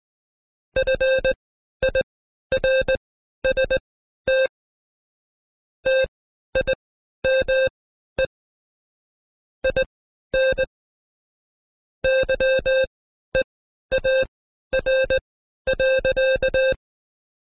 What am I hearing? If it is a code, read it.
Morse code, heard as a keyed tone, about 20 wpm, spaced as for 12 wpm, 530 Hz. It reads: FIRST TIME IN YEAR.
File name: Code Quick / AudioDownld2 / Seg132-Stepper5 11WPM